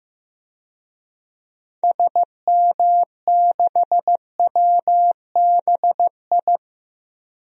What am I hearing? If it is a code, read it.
Morse code, heard as a keyed tone, about 15 wpm, 700 Hz. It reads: SM6WBI
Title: DXCC-100 Call Signs - 6 Characters - Repeated 2x - 15wpm